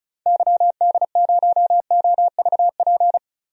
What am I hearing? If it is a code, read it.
Morse code, heard as a keyed tone, about 34 wpm, 700 Hz. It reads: YD0OVP
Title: DXCC-100 Call Signs - 6 Characters - Repeated 3x - 35wpm